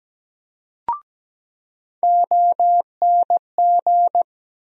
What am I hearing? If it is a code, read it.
Morse code, heard as a keyed tone, about 17 wpm, 700 Hz. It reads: ONG
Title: USA Call Sign Postfixes - 3 Letters 17wpm